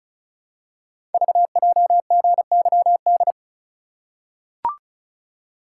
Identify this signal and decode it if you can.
Morse code, heard as a keyed tone, about 36 wpm, 700 Hz. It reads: VJGYD
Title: Code Groups - 5 Letters - 35wpm